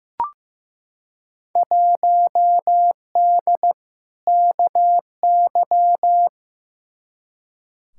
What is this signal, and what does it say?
Morse code, heard as a keyed tone, about 16 wpm, 700 Hz. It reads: 1D KY
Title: Field Day Exchanges - 15wpm